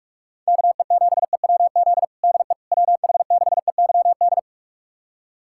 Morse code, heard as a keyed tone, about 46 wpm, 700 Hz. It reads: KE7EWZ DE WH6EYD
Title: USA Callsign DE Callsign 45wpm